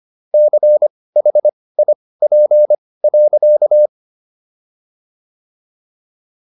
Morse code, heard as a keyed tone, about 25 wpm, spaced as for 18 wpm, 600 Hz. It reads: CHIP.